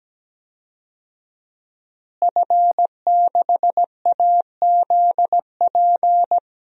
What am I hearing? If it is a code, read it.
Morse code, heard as a keyed tone, about 17 wpm, 700 Hz. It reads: F6AZP
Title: DXCC-100 Call Signs - 5 Characters - Repeated 2x - 17wpm